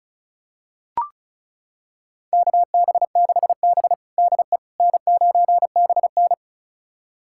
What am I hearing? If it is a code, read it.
Morse code, heard as a keyed tone, about 36 wpm, 700 Hz. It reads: KB6B DE N9BN
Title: USA Callsign DE Callsign - 4 Characters - 35wpm